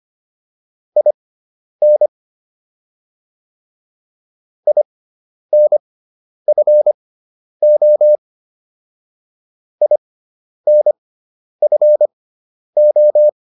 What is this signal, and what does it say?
Morse code, heard as a keyed tone, about 25 wpm, spaced as for 10 wpm, 600 Hz. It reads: IN INFO INFO